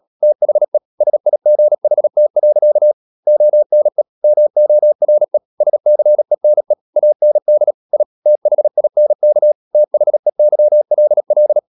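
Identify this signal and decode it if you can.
Morse code, heard as a keyed tone, about 37 wpm, 600 Hz. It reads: THE SIGHT. ONE MORE SCENE AND I THINK THEYLL
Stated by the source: Morse practice at 36 wpm, text of Ken Follett's novel Never, from CW med Gnister.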